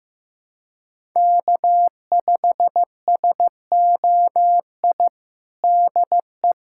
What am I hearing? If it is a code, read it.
Morse code, heard as a keyed tone, about 15 wpm, 700 Hz. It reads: K5SOI DE
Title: USA Callsign DE Callsign 15wpm